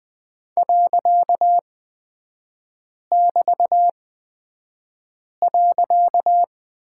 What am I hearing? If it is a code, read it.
Morse code, heard as a keyed tone, about 20 wpm, spaced as for 5 wpm, 700 Hz. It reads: .<BT>.
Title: 255 Lesson 41e - Period - Three Characters 20wpm